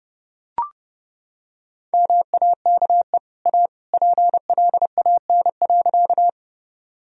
Morse code, heard as a keyed tone, about 32 wpm, 700 Hz. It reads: MAKE A PLAN.
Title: Sentences from Top 1000 Words - Review Part 2 - 30wpm